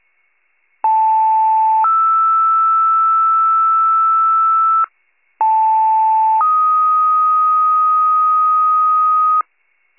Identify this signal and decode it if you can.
Two-tone page, heard as a one-second tone, then a three-second tone, about 880 then 1320 Hz, about 880 then 1250 Hz. It is heard from 0.8 to 9.4 s.
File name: pager / 21 FIRE / wcalert